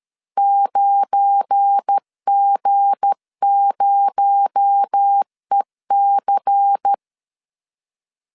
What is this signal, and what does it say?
Morse code, heard as a keyed tone, about 13 wpm, 800 Hz. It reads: 9G0EC
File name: LEZIONI / 7 GF9Z / 7 GF9 12wpm 2319